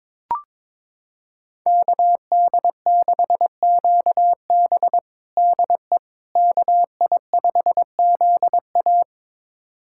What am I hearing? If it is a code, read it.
Morse code, heard as a keyed tone, about 21 wpm, 700 Hz. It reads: KD6QB DE KI5ZA
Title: USA Callsign DE Callsign - 5 Characters - 22wpm